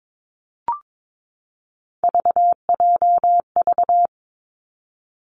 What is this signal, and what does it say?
Morse code, heard as a keyed tone, about 22 wpm, 700 Hz. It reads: VJV
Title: Three Challenging Letter-Number Combinations 22wpm